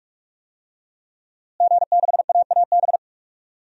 Morse code, heard as a keyed tone, about 43 wpm, 700 Hz. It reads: G6AAB
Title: DXCC-100 Call Signs - 5 Characters - Repeated 2x - 45wpm